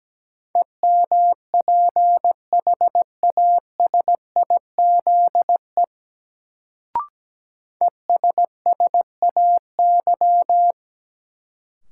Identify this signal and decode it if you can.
Morse code, heard as a keyed tone, about 17 wpm, 700 Hz. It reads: EMPHASIZE ESSAY
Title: Sentences from Top 1500 Words - New Word Drill - 17wpm